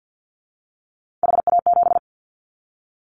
Morse code, heard as a keyed tone, about 50 wpm, 700 Hz. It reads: HA7
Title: Callsign Prefixes for Top 50 DXCC - 50wpm